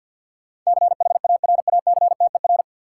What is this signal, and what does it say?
Morse code, heard as a keyed tone, about 53 wpm, 700 Hz. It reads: CHARACTER
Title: Sentences from Top 1200 Words - New Word Drill - 50wpm